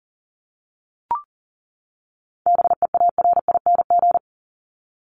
Morse code, heard as a keyed tone, about 40 wpm, 700 Hz. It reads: BEARING
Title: ING Suffix - Top 100 Words with Base Words 40wpm